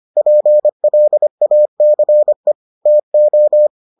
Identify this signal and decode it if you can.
Morse code, heard as a keyed tone, about 25 wpm, 600 Hz. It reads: PLACE TO